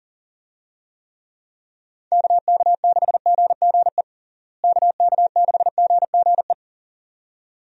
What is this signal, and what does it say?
Morse code, heard as a keyed tone, about 40 wpm, 700 Hz. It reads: KK6GGE KK6GGE
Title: USA Call Signs - 6 Characters - Repeated 2x - 40wpm